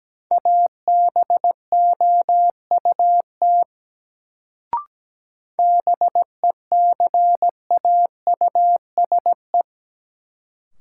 Morse code, heard as a keyed tone, about 17 wpm, 700 Hz. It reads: ABOUT BECAUSE
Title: Sets of 2 Words - New Word Drill - 17wpm